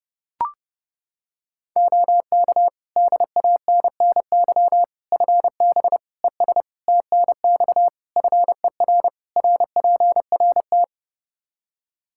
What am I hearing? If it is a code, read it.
Morse code, heard as a keyed tone, about 32 wpm, 700 Hz. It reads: OK DANNY FB ES TNX FER RPRT
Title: Practice QSO - Exchange 2 - Info - 30wpm